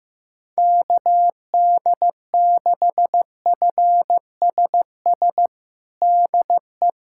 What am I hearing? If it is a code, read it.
Morse code, heard as a keyed tone, about 15 wpm, 700 Hz. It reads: KD6FSS DE
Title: USA Callsign DE Callsign 15wpm